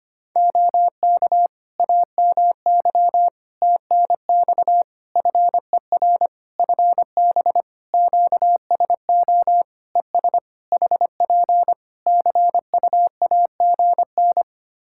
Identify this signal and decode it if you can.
Morse code, heard as a keyed tone, about 25 wpm, 700 Hz. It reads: OK AMY TNX FER FB QSO ES HP CUAGN
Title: Practice QSO - Exchange 4 - Ending - 25wpm